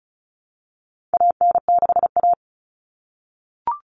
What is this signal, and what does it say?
Morse code, heard as a keyed tone, about 34 wpm, 700 Hz. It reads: AN6A
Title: Common Cut Numbers - 4-Digit 35wpm